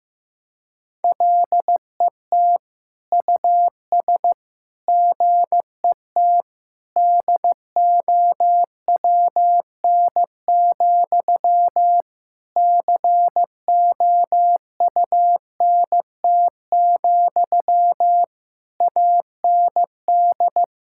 Morse code, heard as a keyed tone, about 15 wpm, 700 Hz. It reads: LET US GET DOWN, COUNT, AND
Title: Sentences from Top 1600 Words 15wpm